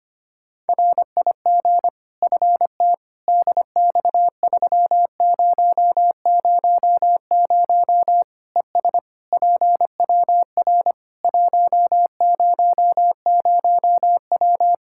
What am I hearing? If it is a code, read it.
Morse code, heard as a keyed tone, about 25 wpm, 700 Hz. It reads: RIG FT DX3000 ES PWR 100W
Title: Practice QSO - Exchange 2 - Info - 25wpm